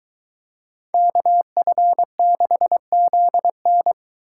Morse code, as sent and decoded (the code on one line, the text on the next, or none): -.- ..-. -.... --.. -.
KF6ZN